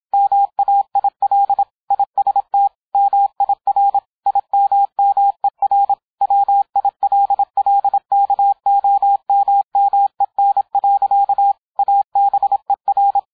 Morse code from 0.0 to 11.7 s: MAIL IST MIR IMMER WILLKOMMEN.